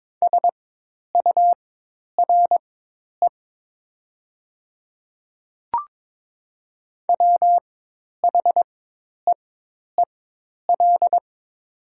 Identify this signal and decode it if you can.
Morse code, heard as a keyed tone, about 22 wpm, spaced as for 10 wpm, 700 Hz. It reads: SURE WHEEL
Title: 4x Character Spacing - Top 500 Words - 22wpm